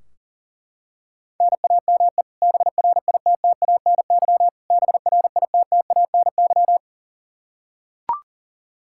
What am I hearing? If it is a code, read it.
Morse code, heard as a keyed tone, about 40 wpm, 700 Hz. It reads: NAME BRITTANY BRITTANY
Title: Practice QSO - Exchange 1 - Beginning - 40wpm